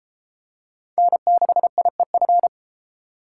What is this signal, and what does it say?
Morse code, heard as a keyed tone, about 28 wpm, 700 Hz. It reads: N6IEF